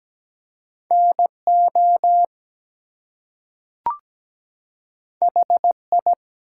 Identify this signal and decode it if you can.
Morse code, heard as a keyed tone, about 17 wpm, 700 Hz. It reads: NO HI
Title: Two Letter Words - 17wpm